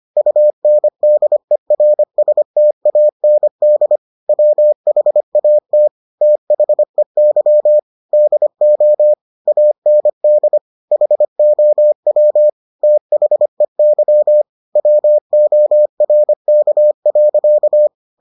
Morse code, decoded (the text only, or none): UNDERSTAND WHAT THEY DO AND HOW THEY WORK.